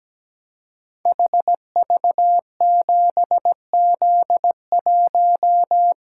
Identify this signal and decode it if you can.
Morse code, heard as a keyed tone, about 17 wpm, 700 Hz. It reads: HV7Z1
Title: Code Groups - 5 Letter:Numbers - 17wpm